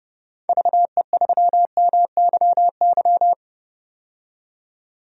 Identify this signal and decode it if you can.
Morse code, heard as a keyed tone, about 31 wpm, 700 Hz. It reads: VE3MYY
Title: DXCC-100 Call Signs - 6 Characters - 30wpm